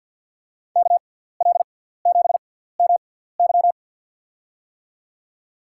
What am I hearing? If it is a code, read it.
Morse code, heard as a keyed tone, about 47 wpm, spaced as for 17 wpm, 700 Hz. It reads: KL7MY
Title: USA Call Signs - 6x Character Spacing - 50wpm